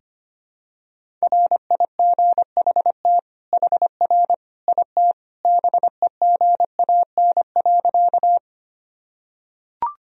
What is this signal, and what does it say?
Morse code, heard as a keyed tone, about 25 wpm, 700 Hz. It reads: RIGHT HR IT BEGAN.
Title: Sentences from Top 300 Words and Abbreviations 25wpm